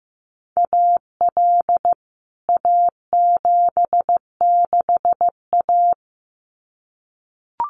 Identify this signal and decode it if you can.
Morse code, heard as a keyed tone, about 15 wpm, 700 Hz. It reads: AL A76A
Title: CWT Exchange 15wpm